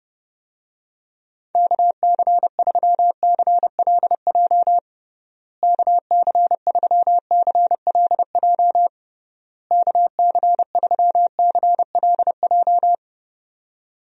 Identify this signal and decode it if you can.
Morse code, heard as a keyed tone, about 30 wpm, 700 Hz. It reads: KC3CLJ KC3CLJ KC3CLJ
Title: USA Call Signs - 6 Characters - Repeated 3x - 30wpm